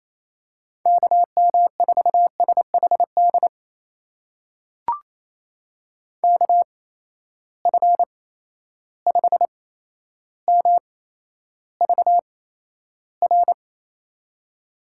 Morse code, heard as a keyed tone, about 28 wpm, spaced as for 12 wpm, 700 Hz. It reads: KM4SHD KF5MVR
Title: USA Call Signs - 6 Characters - 8x Character Spacing - 28wpm